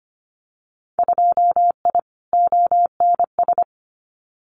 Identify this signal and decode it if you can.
Morse code, heard as a keyed tone, about 25 wpm, 690 Hz. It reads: 2I ONS
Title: Winter Field Day Exchanges Encore 25wpm